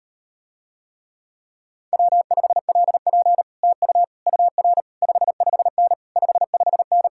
Morse code, heard as a keyed tone, about 38 wpm, 700 Hz. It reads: W5LP TU UR 55N 55N